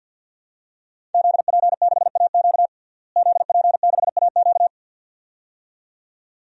Morse code, heard as a keyed tone, about 52 wpm, 700 Hz. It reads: ZP6AQ ZP6AQ
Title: DXCC-100 Call Signs - 5 Characters - Repeated 2x - 50wpm